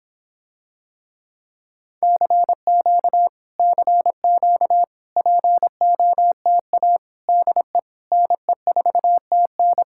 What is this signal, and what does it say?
Morse code, heard as a keyed tone, about 26 wpm, 700 Hz. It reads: CQ CQ POTA DE NE4TN